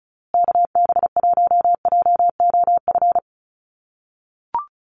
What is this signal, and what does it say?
Morse code, heard as a keyed tone, about 34 wpm, 700 Hz. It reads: KB1JOF
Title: USA Call Signs Repeated 2x 35wpm